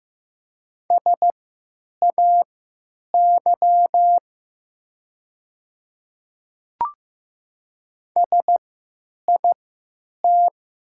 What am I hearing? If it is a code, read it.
Morse code, heard as a keyed tone, about 15 wpm, spaced as for 9 wpm, 700 Hz. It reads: SAY SIT
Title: 3x Character Spacing - Top 500 Words - 15wpm